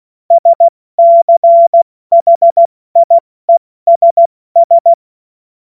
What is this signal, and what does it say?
Morse code, heard as a keyed tone, about 16 wpm, 680 Hz. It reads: SCHIESS
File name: Morsekurs / Witze / Witz 07. September 2021, 70 F80